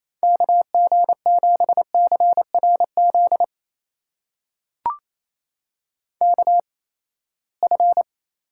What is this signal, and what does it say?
Morse code, heard as a keyed tone, about 28 wpm, spaced as for 19 wpm, 700 Hz. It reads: KG7CRZ KF
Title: USA Call Signs - 6 Characters - 8x Character Spacing - 28wpm